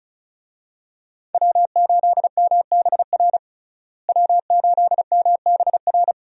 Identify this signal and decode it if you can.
Morse code, heard as a keyed tone, about 34 wpm, 700 Hz. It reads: W8MBR W8MBR
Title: USA Call Signs - 5 Characters - Repeated 2x - 35wpm